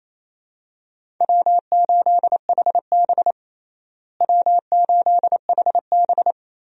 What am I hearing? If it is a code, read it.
Morse code, heard as a keyed tone, about 27 wpm, 700 Hz. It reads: W8HB W8HB
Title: USA Call Signs - 4 Characters - Repeated 3x - 28wpm